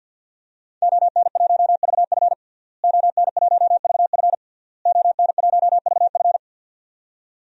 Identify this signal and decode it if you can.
Morse code, heard as a keyed tone, about 49 wpm, 700 Hz. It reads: ON1VF ON1VF ON1VF
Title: DXCC-100 Call Signs - 5 Characters - Repeated 3x - 50wpm